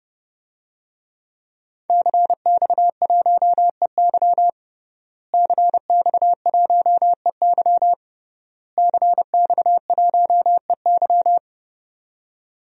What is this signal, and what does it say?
Morse code, heard as a keyed tone, about 31 wpm, 700 Hz. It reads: CX1EY CX1EY CX1EY
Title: DXCC-100 Call Signs - 5 Characters - Repeated 3x - 30wpm